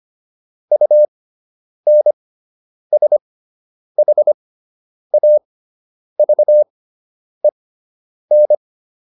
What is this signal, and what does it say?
Morse code, heard as a keyed tone, about 25 wpm, spaced as for 9 wpm, 600 Hz. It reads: UNSHAVEN